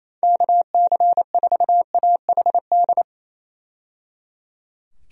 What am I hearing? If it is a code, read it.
Morse code, heard as a keyed tone, about 28 wpm, 700 Hz. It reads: KC4AHD
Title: USA Call Signs - 6 Characters - 28wpm